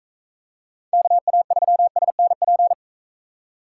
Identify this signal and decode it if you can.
Morse code, heard as a keyed tone, about 40 wpm, 700 Hz. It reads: KA3SNP